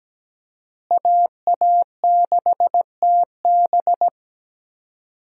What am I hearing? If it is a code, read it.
Morse code, heard as a keyed tone, about 17 wpm, 700 Hz. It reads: AA6TB
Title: USA Call Signs - 5 Characters - Repeated 2x - 17wpm